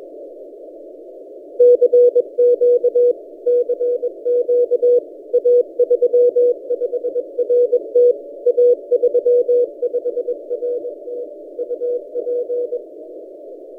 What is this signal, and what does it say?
Morse code, heard as a keyed tone, about 21 wpm, 490 Hz. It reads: CQ CQ A35RT A35RT UP
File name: posts / a35rt-10cw- -0134z